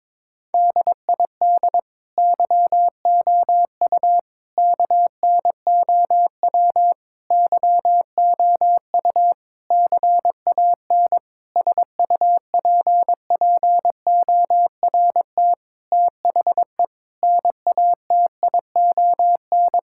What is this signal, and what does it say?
Morse code, heard as a keyed tone, about 21 wpm, 700 Hz. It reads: DID YOU KNOW YOU CAN SUPPORT THE NATION
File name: Sets of 9 Words - 22wpm